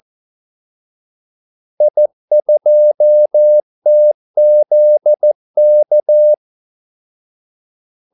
Morse code, heard as a keyed tone, about 14 wpm, 600 Hz. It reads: I2TZK
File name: Callsigns-14wpm